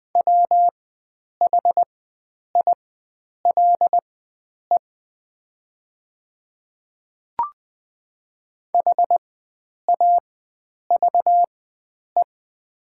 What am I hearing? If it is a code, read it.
Morse code, heard as a keyed tone, about 20 wpm, spaced as for 9 wpm, 700 Hz. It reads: WHILE HAVE
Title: 4x Character Spacing - Top 300 Words - 20wpm